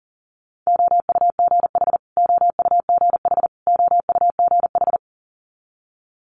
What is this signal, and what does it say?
Morse code, heard as a keyed tone, about 40 wpm, 700 Hz. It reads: OUGH OUGH OUGH
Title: Multi-letter Phonemes Repeated 3x 40wpm